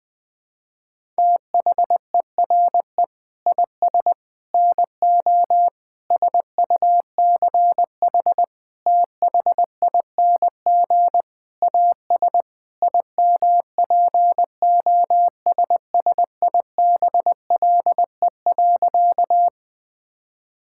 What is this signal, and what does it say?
Morse code, heard as a keyed tone, about 20 wpm, 700 Hz. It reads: THERE IS NO SUCH THING AS IMPOSSIBLE.
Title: Sentences with 6-Word Phrases - 20wpm